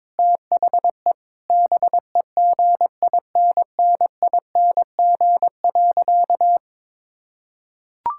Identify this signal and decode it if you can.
Morse code, heard as a keyed tone, about 22 wpm, 700 Hz. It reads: THE BEGINNING.
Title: Sentences from Top 1000 Words - Review Part 5 - 22wpm